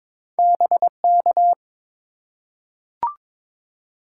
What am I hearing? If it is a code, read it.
Morse code, heard as a keyed tone, about 22 wpm, 700 Hz. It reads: BK
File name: Top 90 QSO Elements - 4x Character Spacing - 22wpm